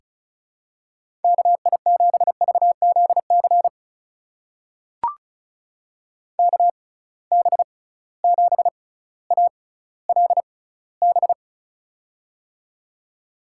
Morse code, heard as a keyed tone, about 36 wpm, spaced as for 18 wpm, 700 Hz. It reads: KI7VZC KB7ALB
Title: USA Call Signs - 6 Characters - 6x Character Spacing - 35wpm